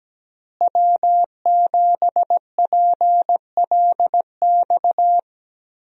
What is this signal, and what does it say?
Morse code, heard as a keyed tone, about 17 wpm, 700 Hz. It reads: W7PLX
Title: USA Call Signs - 5 Characters - Repeated 3x - 17wpm